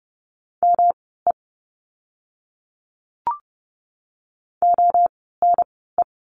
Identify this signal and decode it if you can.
Morse code, heard as a keyed tone, about 29 wpm, spaced as for 17 wpm, 700 Hz. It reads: ME ONE
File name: course / 30:10wpm / 230 Lesson 37f - Letter x - Words 30wpm